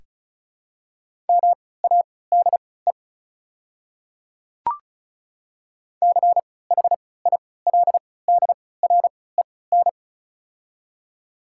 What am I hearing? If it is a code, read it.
Morse code, heard as a keyed tone, about 34 wpm, spaced as for 20 wpm, 700 Hz. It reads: MADE CHILDREN
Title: 3x Character Spacing - Top 500 Words - 35wpm